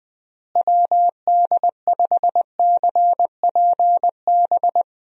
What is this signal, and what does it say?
Morse code, heard as a keyed tone, about 21 wpm, 700 Hz. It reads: WD5CPB
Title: USA Call Signs - 6 Characters - Repeated 2x - 20wpm